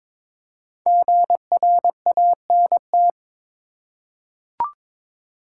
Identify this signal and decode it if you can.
Morse code, heard as a keyed tone, about 22 wpm, 700 Hz. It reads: GRANT
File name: 5 Letter Words - First 3 Letters Given - 22wpm